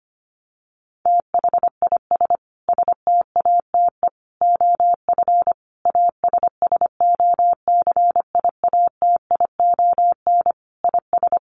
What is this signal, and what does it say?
Morse code, heard as a keyed tone, about 25 wpm, 700 Hz. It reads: THIS STATE OF ASSOCIATION IS
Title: Sentences from Top 1100 Words 25wpm